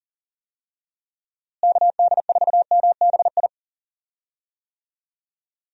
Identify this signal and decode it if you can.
Morse code, heard as a keyed tone, about 40 wpm, 700 Hz. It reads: KD4MBI
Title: USA Call Signs - 6 Characters - Repeated 2x - 40wpm